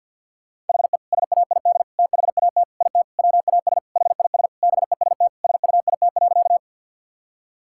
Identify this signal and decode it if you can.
Morse code, heard as a keyed tone, about 50 wpm, 700 Hz. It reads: HE SAID THAT IT WAS HIS BEST SUIT.
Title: Sentences from Top 1600 Words 50wpm